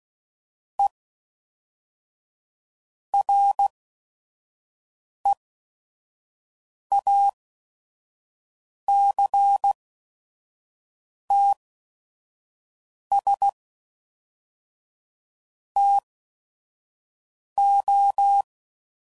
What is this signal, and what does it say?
Morse code, heard as a keyed tone, about 16 wpm, spaced as for 5 wpm, 800 Hz. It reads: E REACTS TO
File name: CW Code Groups @5WPM / Technical Subjects / Reactance